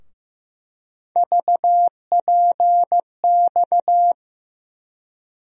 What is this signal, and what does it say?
Morse code, heard as a keyed tone, about 15 wpm, 700 Hz. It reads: VPX
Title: USA Call Sign Postfixes - 3 Letters 15wpm